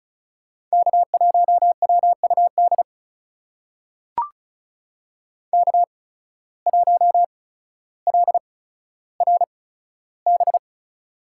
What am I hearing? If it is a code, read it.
Morse code, heard as a keyed tone, about 36 wpm, spaced as for 15 wpm, 700 Hz. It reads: K1WUD K1LRB
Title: USA Call Signs - 5 Characters - 8x Character Spacing - 35wpm